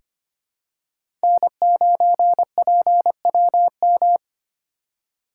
Morse code, heard as a keyed tone, about 25 wpm, 700 Hz. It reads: N9PWM
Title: USA Call Signs - 5 Characters - 2x Character Spacing - 25wpm